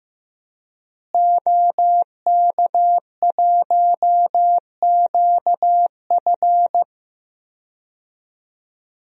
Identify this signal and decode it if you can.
Morse code, heard as a keyed tone, about 15 wpm, 700 Hz. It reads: OK1QF
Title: DXCC-100 Call Signs - 5 Characters - Repeated 2x - 15wpm